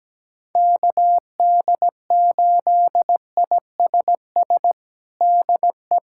Morse code, heard as a keyed tone, about 17 wpm, 700 Hz. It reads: KD8ISS DE
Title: USA Callsign DE Callsign - 6 Characters - 17wpm